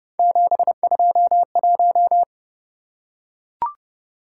Morse code, heard as a keyed tone, about 28 wpm, 700 Hz. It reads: 721